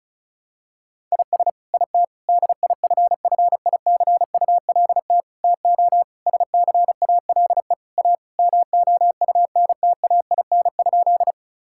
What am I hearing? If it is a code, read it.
Morse code, heard as a keyed tone, about 34 wpm, 700 Hz. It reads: IS IT DIFFICULT TO SCALE A MOUNTAIN?
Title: Sentences from Top 1000 Words - Review Part 5 - 35wpm